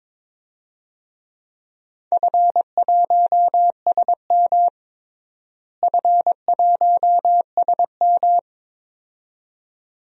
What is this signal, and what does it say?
Morse code, heard as a keyed tone, about 21 wpm, 700 Hz. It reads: F1SM F1SM
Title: DXCC-100 Call Signs - 4 Characters - Repeated 2x - 22wpm